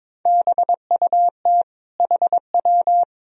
Morse code, heard as a keyed tone, about 22 wpm, 700 Hz. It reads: BUT HW
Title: Sentences from Top 400 Words and Abbreviations 22wpm